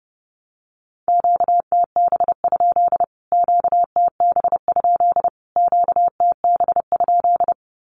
Morse code, heard as a keyed tone, about 30 wpm, 700 Hz. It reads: QTB? QTB? QTB?
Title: Q-Codes Repeated 3x 30wpm